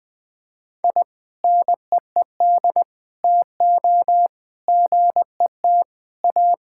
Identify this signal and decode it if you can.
Morse code, heard as a keyed tone, about 21 wpm, 700 Hz. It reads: I NEED TO GET A
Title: Sentences with 2-Word Phrases - 20wpm